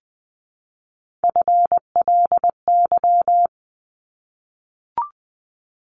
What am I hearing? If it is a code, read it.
Morse code, heard as a keyed tone, about 20 wpm, 700 Hz. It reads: FLY
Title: 260 Lesson 42e - Words 20-20wpm